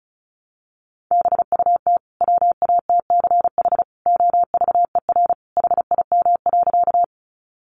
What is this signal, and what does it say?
Morse code, heard as a keyed tone, about 34 wpm, 700 Hz. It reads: BUT WATCH OVER HIM.
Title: Sentences from Top 300 Words 35wpm